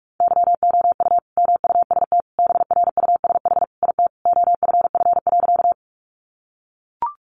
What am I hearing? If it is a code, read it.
Morse code, heard as a keyed tone, about 44 wpm, 700 Hz. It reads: YOU MUST BRUSH IT OFF.
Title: Sentences from Top 1100 Words 45wpm